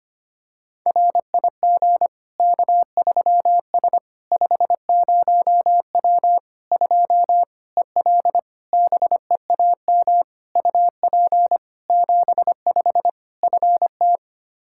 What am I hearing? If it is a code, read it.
Morse code, heard as a keyed tone, about 25 wpm, 700 Hz. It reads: RIG K3S 50W 2 EL BEAM UP 75 FT
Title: Practice QSO - Exchange 2 - Info - 25wpm